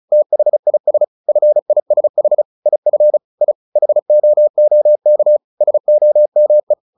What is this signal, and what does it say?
Morse code, heard as a keyed tone, about 34 wpm, 600 Hz. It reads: THIS FISH IF I HOOK SOME
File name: the old man and the sea / the old man and the sea 1 35wpm